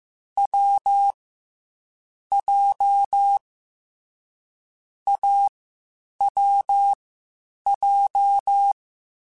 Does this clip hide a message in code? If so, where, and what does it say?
Yes: Morse code from 0.4 to 8.7 s, heard as a keyed tone, about 15 wpm, 800 Hz. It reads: W J A W J